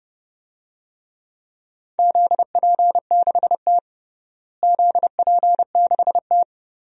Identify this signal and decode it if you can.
Morse code, heard as a keyed tone, about 30 wpm, 700 Hz. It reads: ZP6T ZP6T
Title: DXCC-100 Call Signs - 4 Characters - Repeated 2x - 30wpm